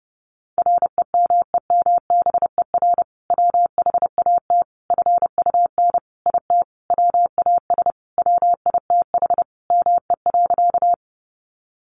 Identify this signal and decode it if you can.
Morse code, heard as a keyed tone, about 30 wpm, 700 Hz. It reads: REMEMBER WHAT FUN IT WAS WITH ME.
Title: Sentences from Top 700 Words 30wpm